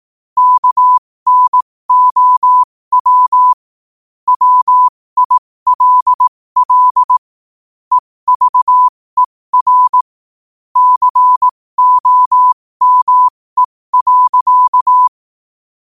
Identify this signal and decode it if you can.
Morse code, heard as a keyed tone, about 18 wpm, spaced as for 15 wpm, 1000 Hz. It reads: KNOW WILL EVER COME.
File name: qotdfiles / 15wpm